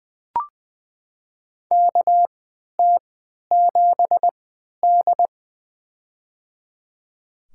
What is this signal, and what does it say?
Morse code, heard as a keyed tone, about 20 wpm, spaced as for 11 wpm, 700 Hz. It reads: KT7D
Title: USA Call Signs - 4 Characters - 3x Character Spacing - 20wpm